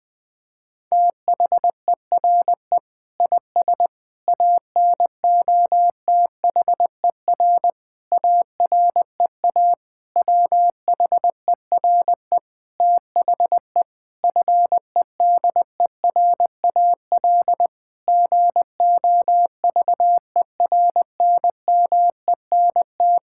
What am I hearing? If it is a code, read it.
Morse code, heard as a keyed tone, about 21 wpm, 700 Hz. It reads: THERE IS ANOTHER AREA WHERE THE FEDERAL GOVERNMENT
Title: Sentences from Top 1000 Words - Review Part 5 - 20wpm